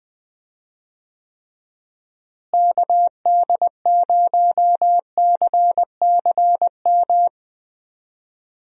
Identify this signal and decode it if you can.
Morse code, heard as a keyed tone, about 20 wpm, 700 Hz. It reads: KD0CCM
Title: USA Call Signs - 6 Characters - Repeated 2x - 20wpm